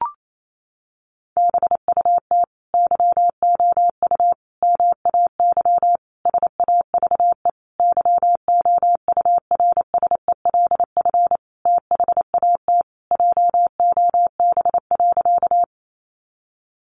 Morse code, heard as a keyed tone, about 28 wpm, 700 Hz. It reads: BUT YOU MAY SAVE YOURSELF THAT JOB.
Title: Sentences from Top 1100 Words 28wpm